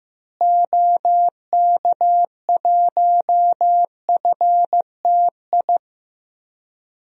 Morse code, heard as a keyed tone, about 15 wpm, 700 Hz. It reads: OK1FTI
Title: DXCC-100 Call Signs - 6 Characters - Repeated 2x - 15wpm